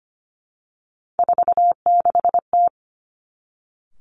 Morse code, heard as a keyed tone, about 25 wpm, 700 Hz. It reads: 46T
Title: Three Challenging Letter-Number Combinations 25wpm